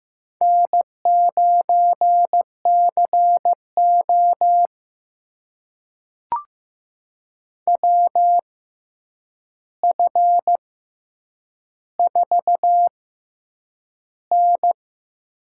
Morse code, heard as a keyed tone, about 15 wpm, spaced as for 8 wpm, 700 Hz. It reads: N9CO WF4N
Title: USA Call Signs - 4 Characters - 6x Character Spacing - 15wpm